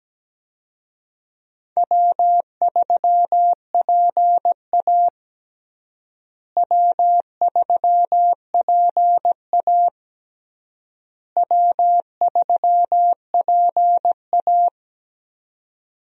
Morse code, heard as a keyed tone, about 17 wpm, 700 Hz. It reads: W3PA W3PA W3PA
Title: USA Call Signs - 4 Characters - Repeated 3x - 17wpm